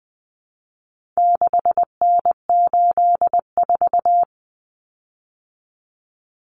Morse code, heard as a keyed tone, about 20 wpm, 700 Hz. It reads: 6N84
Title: Common Cut Numbers - 4-Digit 20wpm